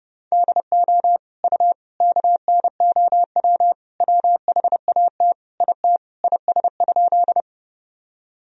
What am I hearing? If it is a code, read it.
Morse code, heard as a keyed tone, about 32 wpm, 700 Hz. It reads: DO U KNOW WHAT IT IS?
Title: Sentences from Top 100 Words and Abbreviations 30wpm